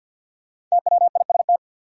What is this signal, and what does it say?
Morse code, heard as a keyed tone, about 52 wpm, 700 Hz. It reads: TWIST
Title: Five Letter Words - 50wpm